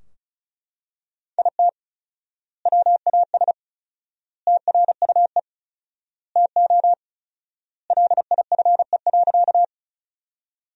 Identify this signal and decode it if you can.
Morse code, heard as a keyed tone, about 34 wpm, 700 Hz. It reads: IT WAS TRUE TO LIFE.
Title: 4x Word-Spacing - Sentences from Top 400 Words - 35wpm